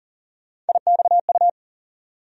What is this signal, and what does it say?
Morse code, heard as a keyed tone, about 40 wpm, 700 Hz. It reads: IXU
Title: USA Call Sign Postfixes - 3 Letters 40wpm